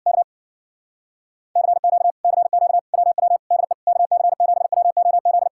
Morse code, heard as a keyed tone, about 55 wpm, 700 Hz. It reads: K CQ CQ WW DE KC7WOZ
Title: Speed-Racing Common QSO Phrases - 50wpm